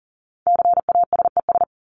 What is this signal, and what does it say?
Morse code, heard as a keyed tone, about 40 wpm, 700 Hz. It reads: CASES
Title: ES:S Suffix - Top 100 Words with Base Words 40wpm